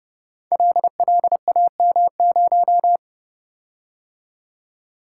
Morse code, heard as a keyed tone, about 30 wpm, 700 Hz. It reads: LLAM0
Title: Code Groups - 5 Letter:Numbers - 30wpm